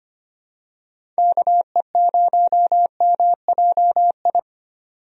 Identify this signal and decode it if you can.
Morse code, heard as a keyed tone, about 25 wpm, 700 Hz. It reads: KE0MJI
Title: USA Call Signs - 6 Characters - 25wpm